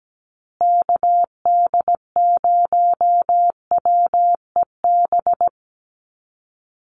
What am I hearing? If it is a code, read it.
Morse code, heard as a keyed tone, about 17 wpm, 700 Hz. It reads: KD0WEB
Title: USA Call Signs Repeated 2x 17wpm